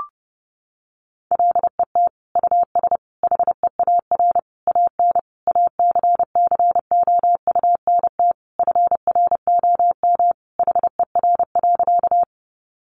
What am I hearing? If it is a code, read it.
Morse code, heard as a keyed tone, about 30 wpm, 700 Hz. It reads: LET US HEAR AN ACCOUNT FROM HER.
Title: Sentences from Top 900 Words 30wpm